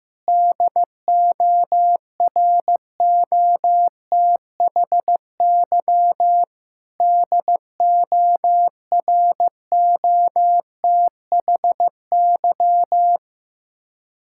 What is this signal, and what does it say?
Morse code, heard as a keyed tone, about 15 wpm, 700 Hz. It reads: DOROTHY DOROTHY
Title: Practice QSO - Exchange 1 - Beginning - 15wpm